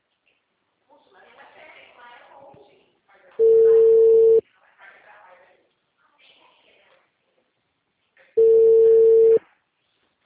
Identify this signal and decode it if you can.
Ringback tone (China, 450 Hz), with speech.